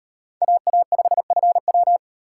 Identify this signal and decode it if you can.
Morse code, heard as a keyed tone, about 38 wpm, 700 Hz. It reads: AA5FW